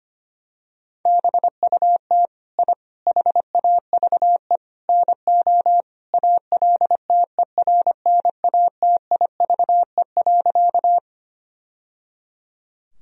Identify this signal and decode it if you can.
Morse code, heard as a keyed tone, about 25 wpm, 700 Hz. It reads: BUT I HAVE NO ALTERNATIVE.
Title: Sentences from Top 1000 Words - Review Part 4 - 25wpm